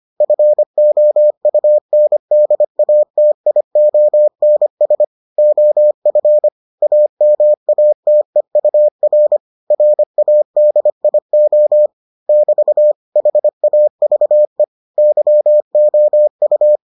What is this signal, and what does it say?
Morse code, heard as a keyed tone, about 25 wpm, 600 Hz. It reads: FOUNDATIONS OF AMATEUR RADIO <BT> HAVE YOU